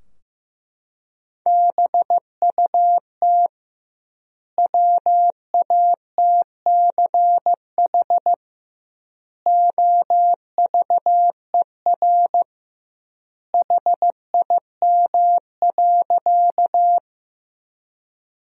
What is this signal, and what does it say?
Morse code, heard as a keyed tone, about 15 wpm, 700 Hz. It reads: BUT WATCH OVER HIM.
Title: 2x Word-Spacing - Sentences from Top 300 Words - 15wpm